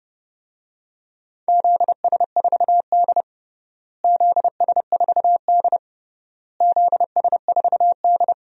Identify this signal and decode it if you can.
Morse code, heard as a keyed tone, about 32 wpm, 700 Hz. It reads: ZS4D ZS4D ZS4D
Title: DXCC-100 Call Signs - 4 Characters - Repeated 3x - 30wpm